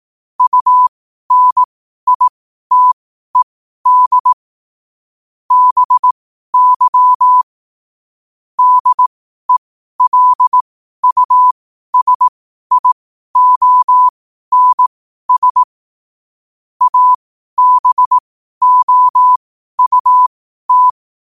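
Morse code, heard as a keyed tone, about 18 wpm, spaced as for 12 wpm, 1000 Hz. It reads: UNITED BY DELUSIONS ABOUT
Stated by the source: Morse code quote of the day at 12 wpm.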